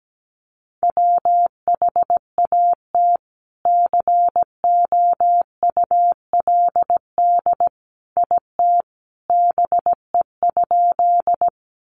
Morse code, heard as a keyed tone, about 17 wpm, 700 Hz. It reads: WHAT COULD IT BE?
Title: Sentences from Top 100 Words 17wpm